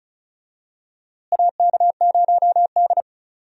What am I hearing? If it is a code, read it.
Morse code, heard as a keyed tone, about 34 wpm, 700 Hz. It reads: AK0D
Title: USA Call Signs - 4 Characters - Repeated 3x - 35wpm